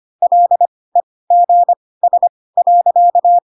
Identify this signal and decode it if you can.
Morse code, heard as a keyed tone, about 25 wpm, spaced as for 18 wpm, 700 Hz. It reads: LEGS.
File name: Pinocchio / Pinocchio ch2 12 25 ik1hge